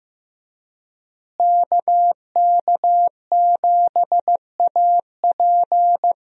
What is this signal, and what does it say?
Morse code, heard as a keyed tone, about 15 wpm, 700 Hz. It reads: KK7AP
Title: USA Call Signs - 5 Characters - Repeated 3x - 15wpm